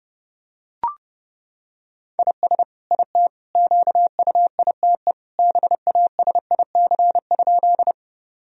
Morse code, heard as a keyed tone, about 30 wpm, 700 Hz. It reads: IS IT QUITE BASIC?
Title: Sentences from Top 1000 Words - Review Part 3 - 30wpm